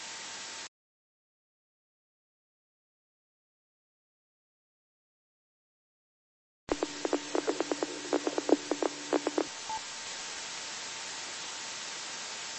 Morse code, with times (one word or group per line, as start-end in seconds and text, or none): none